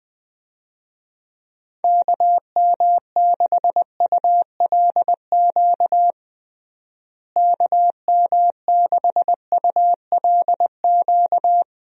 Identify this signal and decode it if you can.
Morse code, heard as a keyed tone, about 21 wpm, 700 Hz. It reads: KM6ULQ KM6ULQ
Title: USA Call Signs - 6 Characters - Repeated 3x - 20wpm